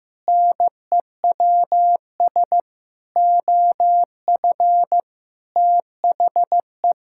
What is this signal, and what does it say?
Morse code, heard as a keyed tone, about 15 wpm, 700 Hz. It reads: NEWS OF THE
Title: Sentences from Top 1600 Words 15wpm